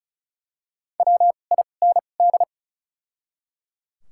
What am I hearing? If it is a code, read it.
Morse code, heard as a keyed tone, about 36 wpm, 700 Hz. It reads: W I N D
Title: 2x Character Spacing - Top 500 Words - 35wpm